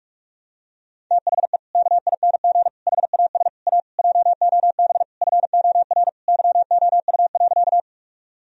Morse code, heard as a keyed tone, about 43 wpm, 700 Hz. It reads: THE KING HAS A JOB FOR YOU.
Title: Sentences from Top 1000 Words - Review Part 3 - 45wpm